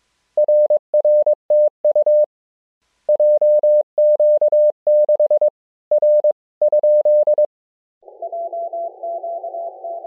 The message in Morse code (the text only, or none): RRTU JQ6 R?